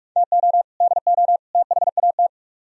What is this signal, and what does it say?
Morse code, heard as a keyed tone, about 46 wpm, 700 Hz. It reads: TO DO THAT